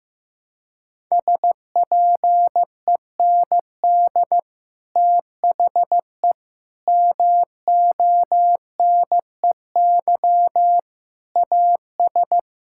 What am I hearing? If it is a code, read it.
Morse code, heard as a keyed tone, about 15 wpm, 700 Hz. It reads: SPEND THE MONEY AS
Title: Sentences from Top 1000 Words - Review Part 5 - 15wpm